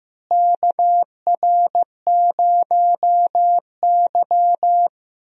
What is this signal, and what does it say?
Morse code, heard as a keyed tone, about 15 wpm, 700 Hz. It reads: KR0Y
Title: USA Call Signs - 4 Characters - Repeated 2x - 15wpm